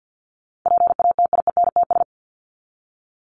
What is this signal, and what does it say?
Morse code, heard as a keyed tone, about 50 wpm, 700 Hz. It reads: PATIENTS
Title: ES:S Suffix - Top 100 Words with Base Words 50wpm